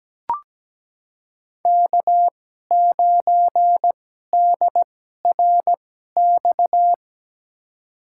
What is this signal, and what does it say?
Morse code, heard as a keyed tone, about 17 wpm, spaced as for 12 wpm, 700 Hz. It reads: K9DRX
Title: USA Call Signs - 5 Characters - 2x Character Spacing - 17wpm